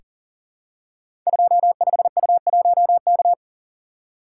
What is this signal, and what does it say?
Morse code, heard as a keyed tone, about 41 wpm, 700 Hz. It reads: 25U1K
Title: Code Groups - 5 Letter:Numbers - 40wpm